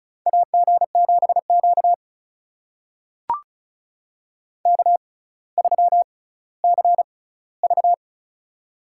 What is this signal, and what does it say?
Morse code, heard as a keyed tone, about 36 wpm, spaced as for 18 wpm, 700 Hz. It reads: AG7Q K3CV
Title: USA Call Signs - 4 Characters - 6x Character Spacing - 35wpm